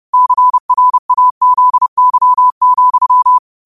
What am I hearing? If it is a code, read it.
Morse code, heard as a keyed tone, about 30 wpm, 1000 Hz. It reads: CRAZY,